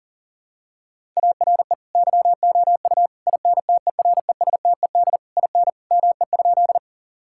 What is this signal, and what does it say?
Morse code, heard as a keyed tone, about 40 wpm, 690 Hz. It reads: ARE YOU INTERESTED IN ME?
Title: Sentences from Top 1000 Words - Review Part 6 - 40wpm